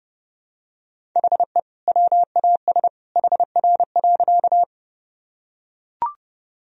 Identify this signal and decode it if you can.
Morse code, heard as a keyed tone, about 32 wpm, 700 Hz. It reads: HE WAS HR.
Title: Sentences from Top 200 Words and Abbreviations 30wpm